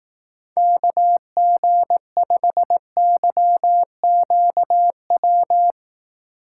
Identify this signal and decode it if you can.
Morse code, heard as a keyed tone, about 18 wpm, 700 Hz. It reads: KG5YQW
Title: Speed Racing - USA 2x3 Call Signs 15wpm